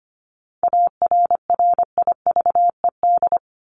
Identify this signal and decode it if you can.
Morse code, heard as a keyed tone, about 25 wpm, 700 Hz. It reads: ARRIVED
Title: ED Suffix - Top 100 Words with Base Words 25wpm